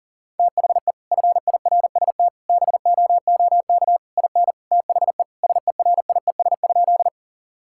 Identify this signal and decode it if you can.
Morse code, heard as a keyed tone, about 40 wpm, 700 Hz. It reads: THE FIRST BOOK IN THE SERIES?
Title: Sentences from Top 1000 Words - Review Part 5 - 40wpm